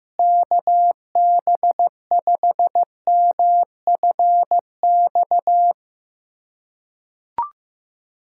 Morse code, heard as a keyed tone, about 15 wpm, 700 Hz. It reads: KB5MFX
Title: USA Callsign DE Callsign - 6 Characters - 15wpm